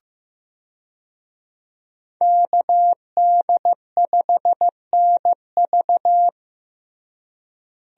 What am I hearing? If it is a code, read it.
Morse code, heard as a keyed tone, about 15 wpm, 700 Hz. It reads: KD5NV
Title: USA Call Signs - 5 Characters - Repeated 3x - 15wpm